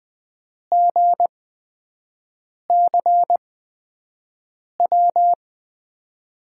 Morse code, heard as a keyed tone, about 20 wpm, spaced as for 5 wpm, 700 Hz. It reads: GCW
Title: USA Call Signs - 6 Characters - 8x Character Spacing - 20wpm